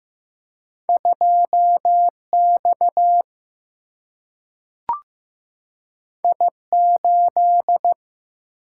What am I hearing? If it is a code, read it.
Morse code, heard as a keyed tone, about 15 wpm, 700 Hz. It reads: 2 X I 8
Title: Single-Letter Single-Number 15wpm